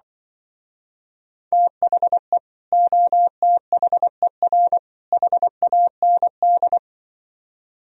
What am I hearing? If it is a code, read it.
Morse code, heard as a keyed tone, about 25 wpm, 700 Hz. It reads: THE OTHER HAND